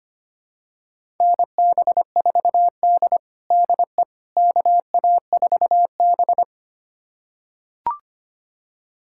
Morse code, heard as a keyed tone, about 25 wpm, 700 Hz. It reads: NB4D DE KA4B
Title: USA Callsign DE Callsign - 4 Characters - 25wpm